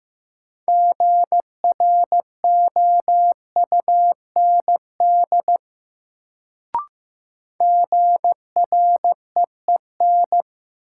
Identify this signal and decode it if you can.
Morse code, heard as a keyed tone, about 15 wpm, 700 Hz. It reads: GROUND GREEN
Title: Sentences from Top 500 Words - New Word Drill - 15wpm